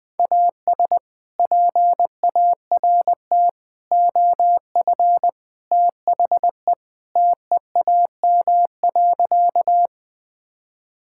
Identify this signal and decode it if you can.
Morse code, heard as a keyed tone, about 20 wpm, 700 Hz. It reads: AS PART OF THE TEAM.
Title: Sentences with 3-Word Phrases - 20wpm